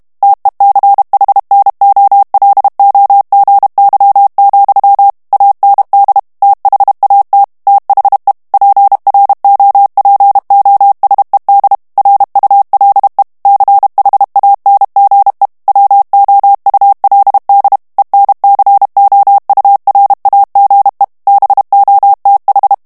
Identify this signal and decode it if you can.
Morse code, heard as a keyed tone, about 32 wpm, 800 Hz. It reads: TECHNOLOGY, AND THAT THE PROPOSED RULE CHANGE WOULD ENCOURAGE BOTH
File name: morse mp3 / ANGLAIS texte clair 3 vitesse 30 mots minute